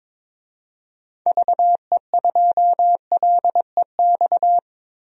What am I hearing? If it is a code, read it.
Morse code, heard as a keyed tone, about 22 wpm, 700 Hz. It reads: VE2LEX
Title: DXCC-100 Call Signs - 6 Characters - Repeated 2x - 22wpm